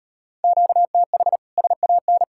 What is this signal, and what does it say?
Morse code, heard as a keyed tone, about 38 wpm, 700 Hz. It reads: QTH SAN